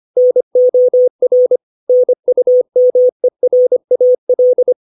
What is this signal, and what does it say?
Morse code, heard as a keyed tone, about 25 wpm, 500 Hz. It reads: NOR NUMERAL
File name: 632 words / 632 Words 25WPM